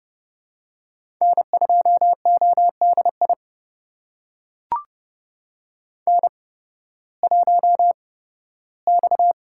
Morse code, heard as a keyed tone, about 30 wpm, spaced as for 16 wpm, 700 Hz. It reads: N2ODI N1X
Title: USA Call Signs - 5 Characters - 8x Character Spacing - 30wpm